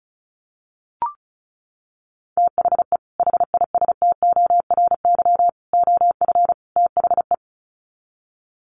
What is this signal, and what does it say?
Morse code, heard as a keyed tone, about 34 wpm, 700 Hz. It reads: THE HISTORY OF THE
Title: Sets of 4 Words 35wpm